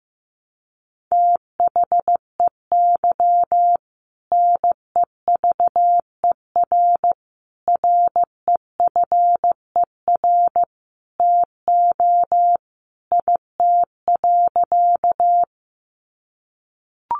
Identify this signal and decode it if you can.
Morse code, heard as a keyed tone, about 15 wpm, 700 Hz. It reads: THEY NEVER REFER TO IT.
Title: Sentences from Top 1000 Words 15wpm